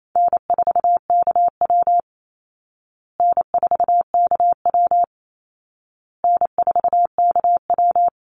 Morse code, heard as a keyed tone, about 28 wpm, 700 Hz. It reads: N4KW N4KW N4KW
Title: Easy Call Signs Repeated 3x - 28wpm